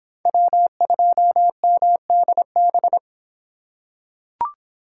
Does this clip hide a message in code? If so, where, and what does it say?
Yes: Morse code from 0.3 to 3.0 s, heard as a keyed tone, about 23 wpm, 700 Hz. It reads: W2MDB